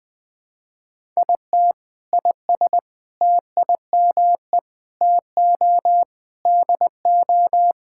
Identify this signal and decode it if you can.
Morse code, heard as a keyed tone, about 20 wpm, 700 Hz. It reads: IT IS TIME TO DO
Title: Sentences with 5-Word Phrases - 20wpm